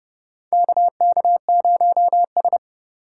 Morse code, heard as a keyed tone, about 30 wpm, 700 Hz. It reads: KK0S
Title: USA Call Signs - 4 Characters - 2x Character Spacing - 30wpm